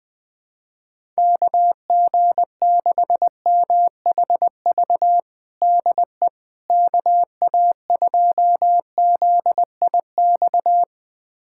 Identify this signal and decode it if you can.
Morse code, heard as a keyed tone, about 20 wpm, 700 Hz. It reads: KG6MHV DE KA2ZIX
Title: USA Callsign DE Callsign 20wpm